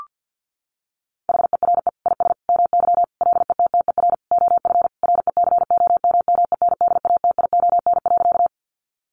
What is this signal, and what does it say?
Morse code, heard as a keyed tone, about 50 wpm, 700 Hz. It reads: HERE IS MY LETTER OF RECOMMENDATION.
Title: Sentences from Top 1400 Words 50wpm